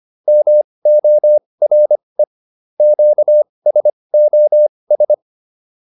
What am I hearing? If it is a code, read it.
Morse code, heard as a keyed tone, about 25 wpm, spaced as for 20 wpm, 600 Hz. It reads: MORE QSOS